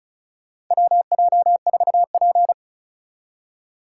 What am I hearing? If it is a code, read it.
Morse code, heard as a keyed tone, about 36 wpm, 700 Hz. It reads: WJ4P
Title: USA Call Signs - 4 Characters - 3x Character Spacing - 35wpm